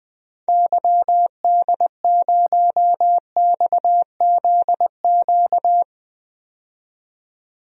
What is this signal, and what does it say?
Morse code, heard as a keyed tone, about 20 wpm, 700 Hz. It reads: YD0XZQ
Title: DXCC-100 Call Signs - 6 Characters - 20wpm